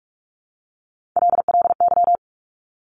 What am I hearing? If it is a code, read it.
Morse code, heard as a keyed tone, about 46 wpm, 700 Hz. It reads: LLY
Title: Top Three Letter Combinations in Words 45wpm